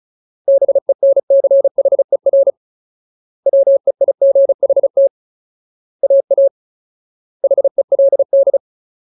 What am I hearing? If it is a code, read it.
Morse code, heard as a keyed tone, about 35 wpm, 550 Hz. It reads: BENCHER WEIGHT AA HELD
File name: Ultimate Word List / 35WPM Ultimate